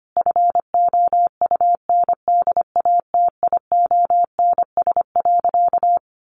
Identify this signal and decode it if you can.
Morse code, heard as a keyed tone, about 25 wpm, 700 Hz. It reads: FOUNDATIONS.
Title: A Sound Of Thunder 25wpm